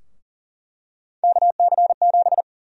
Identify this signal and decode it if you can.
Morse code, heard as a keyed tone, about 40 wpm, 700 Hz. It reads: KC7
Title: USA Call Sign Prefixes - 40wpm